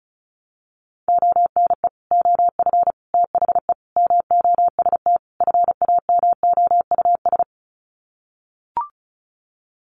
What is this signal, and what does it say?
Morse code, heard as a keyed tone, about 34 wpm, 700 Hz. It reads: ONE OF THE MOST FAMOUS
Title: Sets of 5 Words Encore2 35wpm